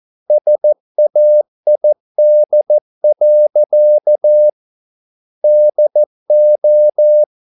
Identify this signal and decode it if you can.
Morse code, heard as a keyed tone, about 14 wpm, 600 Hz. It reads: SAID. DO